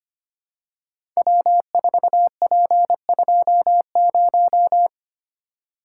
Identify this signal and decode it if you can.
Morse code, heard as a keyed tone, about 25 wpm, 700 Hz. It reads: W4P20
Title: Code Groups - 5 Letter:Numbers - 25wpm